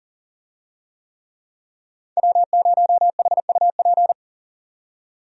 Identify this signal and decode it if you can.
Morse code, heard as a keyed tone, about 40 wpm, 700 Hz. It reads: W0HUP
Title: USA Call Signs - 5 Characters - 40wpm